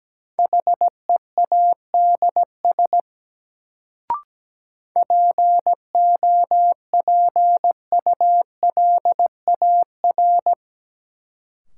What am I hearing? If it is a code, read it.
Morse code, heard as a keyed tone, about 17 wpm, 700 Hz. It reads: HEADS POPULAR
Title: Sentences from Top 1000 Words - Review Part 3 - New Word Drill - 17wpm